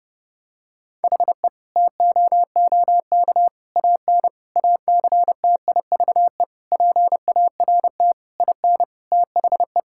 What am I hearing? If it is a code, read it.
Morse code, heard as a keyed tone, about 30 wpm, 700 Hz. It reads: HE TOOK AN ACTIVE PART IN THE
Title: Sentences from Top 1000 Words - Review Part 6 - 30wpm